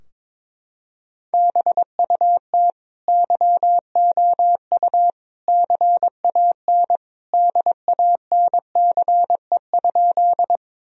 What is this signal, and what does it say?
Morse code, heard as a keyed tone, about 22 wpm, 700 Hz. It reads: BUT YOU CAN DANCE?
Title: Sentences from Top 1000 Words - Review Part 4 - 22wpm